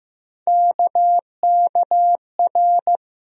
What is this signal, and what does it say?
Morse code, heard as a keyed tone, about 15 wpm, 700 Hz. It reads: KKR
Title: USA Call Sign Postfixes - 3 Letters 15wpm